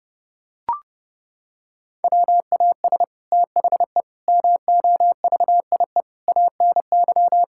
Morse code, heard as a keyed tone, about 30 wpm, 700 Hz. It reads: WAS THE MOVIE ANY
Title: Sentences from Top 1000 Words - Review Part 5 - 30wpm